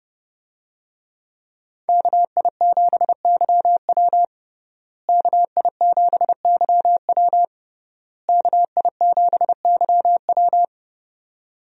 Morse code, heard as a keyed tone, about 30 wpm, 700 Hz. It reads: KI7YW KI7YW KI7YW
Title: USA Call Signs - 5 Characters - Repeated 3x - 30wpm